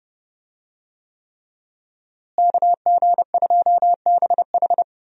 Morse code, heard as a keyed tone, about 30 wpm, 700 Hz. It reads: KG2BH
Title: USA Call Signs - 5 Characters - Repeated 3x - 30wpm